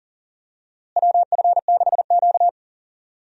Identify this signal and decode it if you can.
Morse code, heard as a keyed tone, about 38 wpm, 700 Hz. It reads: WF6Q
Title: USA Call Signs - 4 Characters - Repeated 3x - 40wpm